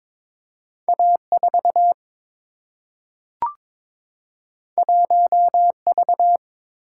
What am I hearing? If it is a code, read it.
Morse code, heard as a keyed tone, about 22 wpm, 700 Hz. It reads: A4 1V
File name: practice / 22wpm / Two Letter-Number Combinations - 22wpm